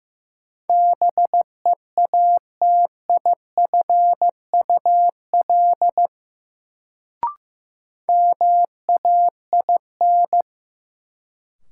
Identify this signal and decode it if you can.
Morse code, heard as a keyed tone, about 15 wpm, 700 Hz. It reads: BEATIFUL MAIN
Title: Sentences from Top 1300 Words - New Word Drill Alt - 15wpm